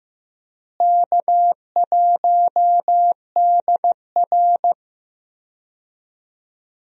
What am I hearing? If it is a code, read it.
Morse code, heard as a keyed tone, about 15 wpm, 700 Hz. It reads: K1DR
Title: USA Call Signs - 4 Characters - Repeated 2x - 15wpm